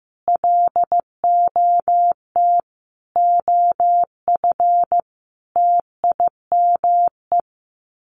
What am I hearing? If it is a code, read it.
Morse code, heard as a keyed tone, about 15 wpm, 700 Hz. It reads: LOT OF TIME
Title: Sets of 5 Words 15wpm